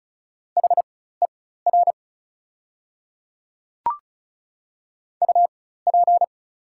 Morse code, heard as a keyed tone, about 36 wpm, spaced as for 16 wpm, 700 Hz. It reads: HER UP
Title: 4x Character Spacing - Top 100 Words - 35wpm